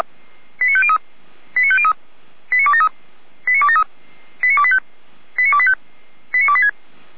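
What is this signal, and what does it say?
Five-tone selective call (ZVEI), heard as quick runs of five tones, tones of about 70 ms. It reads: 89503 89503 89273 89273 89285 89285 89286
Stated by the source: ZVEI five-tone selective call 89280 received by SDR.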